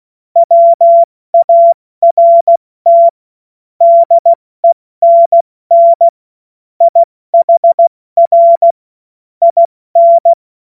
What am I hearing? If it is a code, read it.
Morse code, heard as a keyed tone, about 16 wpm, 680 Hz. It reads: WART DENN IHR IN
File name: Morsekurs / Witze / Witz 19. Oktober 2021, 70 F80